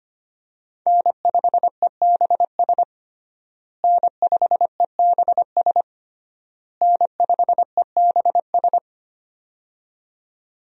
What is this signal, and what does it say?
Morse code, heard as a keyed tone, about 25 wpm, 700 Hz. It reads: N5EBS N5EBS N5EBS
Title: USA Call Signs - 5 Characters - Repeated 3x - 25wpm